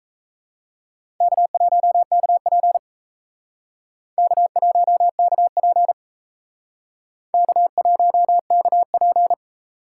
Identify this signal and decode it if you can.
Morse code, heard as a keyed tone, about 37 wpm, 700 Hz. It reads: K1KP K1KP K1KP